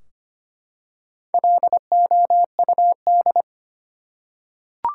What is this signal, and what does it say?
Morse code, heard as a keyed tone, about 25 wpm, 700 Hz. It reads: LOUD
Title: Four Letter Words - 25wpm